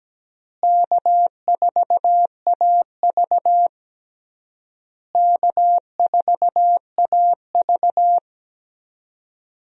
Morse code, heard as a keyed tone, about 17 wpm, 700 Hz. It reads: K4AV K4AV
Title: USA Call Signs - 4 Characters - Repeated 3x - 17wpm